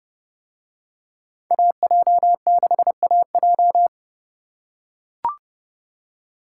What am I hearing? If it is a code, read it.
Morse code, heard as a keyed tone, about 29 wpm, 700 Hz. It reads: AJ6AJ